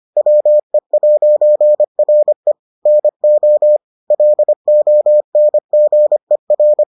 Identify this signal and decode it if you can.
Morse code, heard as a keyed tone, about 25 wpm, 600 Hz. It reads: WE'RE NO LONGER